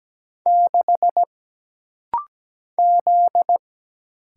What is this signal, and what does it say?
Morse code, heard as a keyed tone, about 17 wpm, 700 Hz. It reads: 6 Z
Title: Single Letter-Number - Rapid-Fire 17wpm